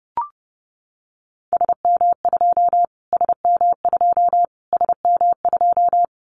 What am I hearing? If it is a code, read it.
Morse code, heard as a keyed tone, about 30 wpm, 700 Hz. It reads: SM2 SM2 SM2
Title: Callsign Prefixes for Top 25 DXCC - Repeated 3x - 30wpm